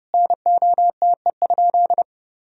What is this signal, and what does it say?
Morse code, heard as a keyed tone, about 31 wpm, 700 Hz. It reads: NOTE?
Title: Sentences from Top 500 Words and Abbreviations 30wpm